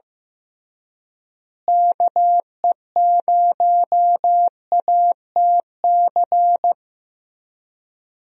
Morse code, heard as a keyed tone, about 15 wpm, 700 Hz. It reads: KE0ATC
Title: USA Call Signs - 6 Characters - Repeated 3x - 15wpm